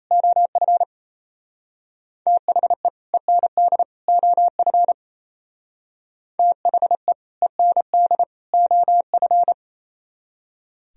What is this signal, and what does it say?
Morse code, heard as a keyed tone, about 33 wpm, 700 Hz. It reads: OF THE END OF THE END OF
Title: Speed-Racing - Sets of 3 Words 28wpm